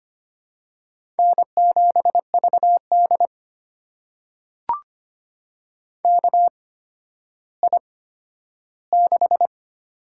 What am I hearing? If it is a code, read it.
Morse code, heard as a keyed tone, about 25 wpm, spaced as for 12 wpm, 700 Hz. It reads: N7VD KI6
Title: USA Call Signs - 8x Character Spacing - 25wpm